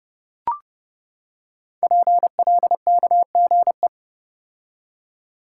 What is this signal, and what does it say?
Morse code, heard as a keyed tone, about 30 wpm, 700 Hz. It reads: PLKGE
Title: Code Groups - 5 Letters - 30wpm